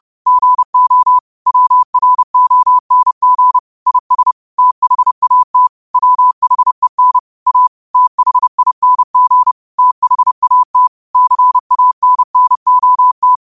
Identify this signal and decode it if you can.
Morse code, heard as a keyed tone, about 30 wpm, 1000 Hz. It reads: GO WRONG IS THAT WHEN A THING THAT CANNOT
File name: qotdfiles / 30wpm